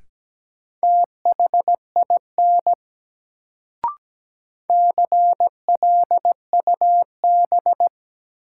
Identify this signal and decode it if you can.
Morse code, heard as a keyed tone, about 17 wpm, 700 Hz. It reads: THIN CLUB
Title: Sentences from Top 1200 Words - New Word Drill - 17wpm